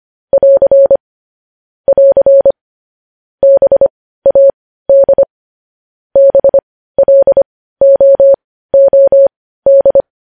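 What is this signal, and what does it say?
Morse code, heard as a keyed tone, about 25 wpm, spaced as for 15 wpm, 550 Hz. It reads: <AR> <AR> BAD BLOOD
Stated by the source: Morse practice sending news headlines at 15 wpm.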